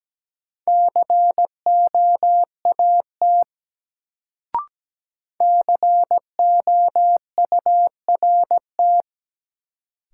Morse code, heard as a keyed tone, about 17 wpm, 700 Hz. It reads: COAT COURT
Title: Sentences from Top 1600 Words - New Word Drill - 17wpm